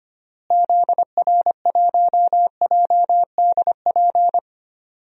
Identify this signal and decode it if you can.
Morse code, heard as a keyed tone, about 25 wpm, 700 Hz. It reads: ZR1JDP
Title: DXCC-100 Call Signs - 6 Characters - Repeated 2x - 25wpm